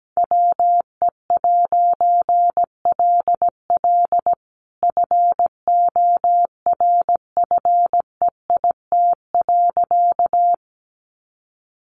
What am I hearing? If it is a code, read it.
Morse code, heard as a keyed tone, about 17 wpm, 700 Hz. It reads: WE'LL FORFEIT.
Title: A Sound Of Thunder 17wpm